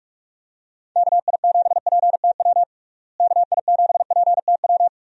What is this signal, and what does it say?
Morse code, heard as a keyed tone, about 46 wpm, 700 Hz. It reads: KI7PTW KI7PTW
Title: USA Call Signs - 6 Characters - Repeated 2x - 45wpm